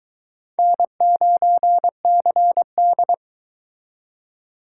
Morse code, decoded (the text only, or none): N9CD